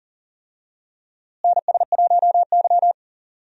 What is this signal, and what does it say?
Morse code, heard as a keyed tone, about 40 wpm, 700 Hz. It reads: NS1Y
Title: USA Call Signs - 4 Characters - Repeated 3x - 40wpm